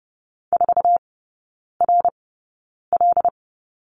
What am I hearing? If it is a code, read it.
Morse code, heard as a keyed tone, about 30 wpm, spaced as for 9 wpm, 700 Hz. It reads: 4RL
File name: Lesson 18e - Letter m - Three Characters 30wpm